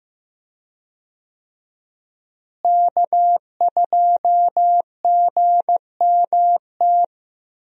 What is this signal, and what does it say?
Morse code, heard as a keyed tone, about 15 wpm, 700 Hz. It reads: K2GMT
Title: USA Call Signs - 5 Characters - Repeated 3x - 15wpm